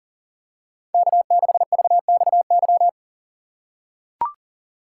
Morse code, heard as a keyed tone, about 37 wpm, 700 Hz. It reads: K6VXY
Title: USA Call Signs - 5 Characters - 2x Character Spacing - 40wpm